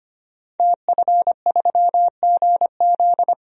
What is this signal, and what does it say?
Morse code, heard as a keyed tone, about 25 wpm, 700 Hz. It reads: TF3GZ
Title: DXCC-100 Call Signs - 5 Characters - Repeated 3x - 25wpm